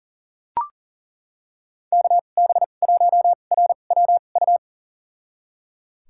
Morse code, heard as a keyed tone, about 38 wpm, spaced as for 28 wpm, 700 Hz. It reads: KB1RWU
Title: USA Call Signs - 6 Characters - 2x Character Spacing - 40wpm